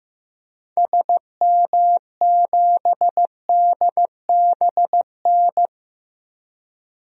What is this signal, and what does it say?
Morse code, heard as a keyed tone, about 15 wpm, 700 Hz. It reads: SM7DBN
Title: DXCC-100 Call Signs - 6 Characters - Repeated 3x - 15wpm